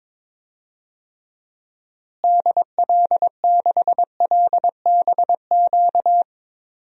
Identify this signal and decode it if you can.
Morse code, heard as a keyed tone, about 22 wpm, 700 Hz. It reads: DL6LBQ
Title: DXCC-100 Call Signs - 6 Characters - Repeated 3x - 22wpm